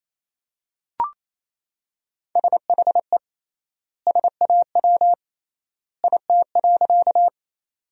Morse code, heard as a keyed tone, about 28 wpm, 700 Hz. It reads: SHE SAW IT.
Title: 3x Word-Spacing - Sentences from Top 300 Words - 28wpm